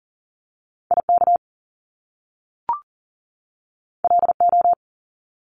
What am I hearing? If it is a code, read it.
Morse code, heard as a keyed tone, about 42 wpm, 700 Hz. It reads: IK LO
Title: Top Two Letter Combinations in Words 40wpm